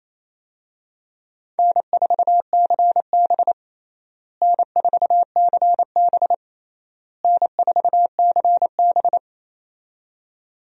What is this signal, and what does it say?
Morse code, heard as a keyed tone, about 28 wpm, 700 Hz. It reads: N4CB N4CB N4CB
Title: USA Call Signs - 4 Characters - Repeated 3x - 28wpm